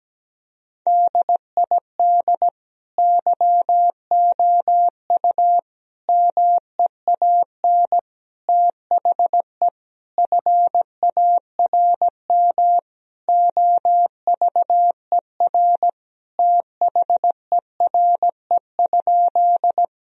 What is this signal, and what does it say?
Morse code, heard as a keyed tone, about 17 wpm, 700 Hz. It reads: DID YOU MEAN THE FARM OVER THERE?
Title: Sentences from Top 1000 Words - Review Part 2 - 17wpm